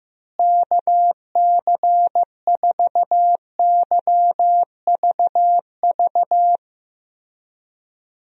Morse code, heard as a keyed tone, about 15 wpm, 700 Hz. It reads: KC4YVV
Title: USA Call Signs - 6 Characters - Repeated 3x - 15wpm